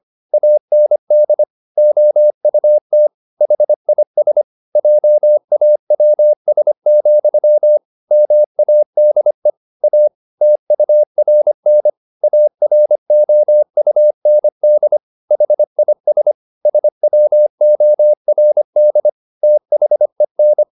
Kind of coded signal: Morse code